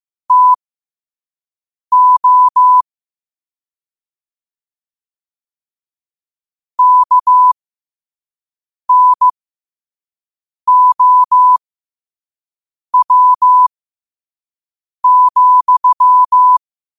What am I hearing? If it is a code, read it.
Morse code, heard as a keyed tone, about 15 wpm, spaced as for 5 wpm, 1000 Hz. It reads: TO KNOW,
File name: qotdfiles / 5wpm